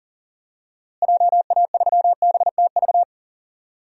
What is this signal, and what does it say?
Morse code, heard as a keyed tone, about 40 wpm, 700 Hz. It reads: JA3BTV
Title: DXCC-100 Call Signs - 6 Characters - Repeated 3x - 40wpm